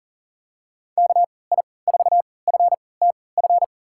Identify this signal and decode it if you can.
Morse code, heard as a keyed tone, about 40 wpm, spaced as for 23 wpm, 700 Hz. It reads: KI4FTF
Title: USA Call Signs - 6 Characters - 3x Character Spacing - 40wpm